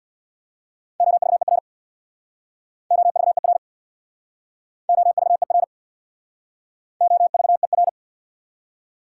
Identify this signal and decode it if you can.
Morse code, heard as a keyed tone, about 66 wpm, 690 Hz. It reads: OVER OVER OVER OVER